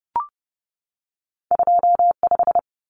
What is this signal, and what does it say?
Morse code, heard as a keyed tone, about 30 wpm, 700 Hz. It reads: 25
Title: Common Cut Numbers - 2-Digit 30wpm